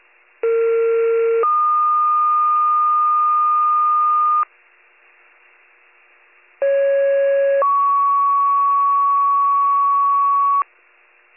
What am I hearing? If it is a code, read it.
Two-tone page, heard as a one-second tone, then a three-second tone, about 460 then 1220 Hz, about 550 then 1120 Hz. It is heard from 0.4 to 10.6 s.